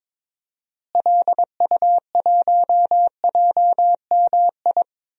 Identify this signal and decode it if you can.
Morse code, heard as a keyed tone, about 22 wpm, 700 Hz. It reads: LU1JMI
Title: DXCC-100 Call Signs - 6 Characters - Repeated 2x - 22wpm